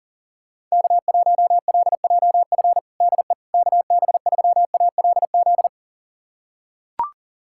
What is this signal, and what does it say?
Morse code, heard as a keyed tone, about 40 wpm, 700 Hz. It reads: K1LJF DE KB3ALZ
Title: USA Callsign DE Callsign 40wpm